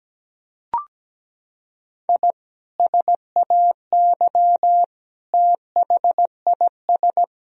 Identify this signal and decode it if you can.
Morse code, heard as a keyed tone, about 17 wpm, 700 Hz. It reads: I SAY THIS
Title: Sentences from Top 1000 Words - Review Part 3 - 17wpm